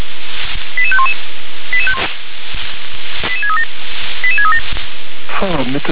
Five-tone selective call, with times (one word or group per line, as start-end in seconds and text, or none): 0.8-1.1 88511
3.3-3.6 88637
4.2-4.6 88637